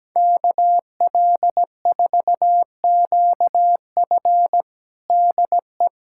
Morse code, heard as a keyed tone, about 17 wpm, 700 Hz. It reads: KL4QF DE
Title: USA Callsign DE Callsign 17wpm